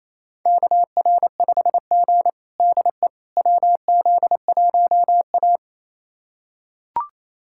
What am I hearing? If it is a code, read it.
Morse code, heard as a keyed tone, about 28 wpm, 700 Hz. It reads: KR5G DE WZ1A
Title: USA Callsign DE Callsign - 4 Characters - 28wpm